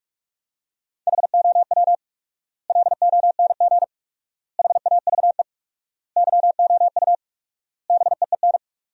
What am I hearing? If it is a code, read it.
Morse code, heard as a keyed tone, about 47 wpm, 700 Hz. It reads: HOW LONG HAVE YOU BEEN
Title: 4x Word-Spacing - Sentences from Top 200 Words - 45wpm